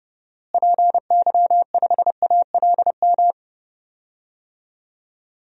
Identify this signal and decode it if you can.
Morse code, heard as a keyed tone, about 32 wpm, 700 Hz. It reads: PY5ALM
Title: DXCC-100 Call Signs - 6 Characters - Repeated 3x - 30wpm